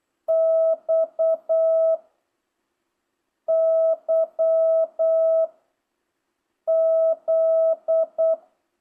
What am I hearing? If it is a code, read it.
Morse code, heard as a keyed tone, about 8 wpm, 660 Hz. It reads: X Y Z